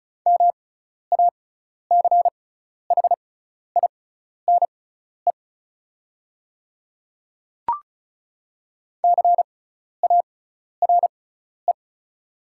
Morse code, heard as a keyed tone, about 34 wpm, spaced as for 12 wpm, 700 Hz. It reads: MACHINE CARE
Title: 6x Character Spacing - Top 500 Words - 35wpm